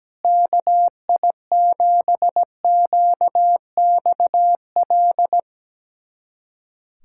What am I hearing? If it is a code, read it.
Morse code, heard as a keyed tone, about 17 wpm, 700 Hz. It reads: KI7QXL
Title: USA Callsign DE Callsign - 6 Characters - 17wpm